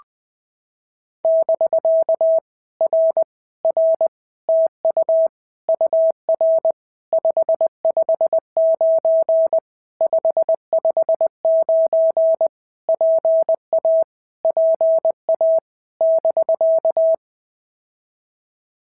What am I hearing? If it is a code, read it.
Morse code, heard as a keyed tone, about 20 wpm, 650 Hz. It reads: <BK> R R TU UR 559 559 PA PA <BK>
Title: POTA QSO Practice - 20wpm